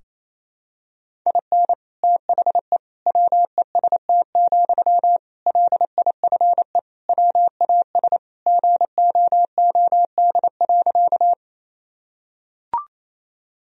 Morse code, heard as a keyed tone, about 28 wpm, 700 Hz. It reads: IN THE WEST, LIFE WAS GOOD.
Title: Sentences from Top 1000 Words - Review Part 2 - 28wpm